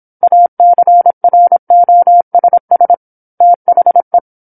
Morse code, heard as a keyed tone, about 27 wpm, 700 Hz. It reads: ACROSS THE
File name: 08-Chapter-08 26 WPM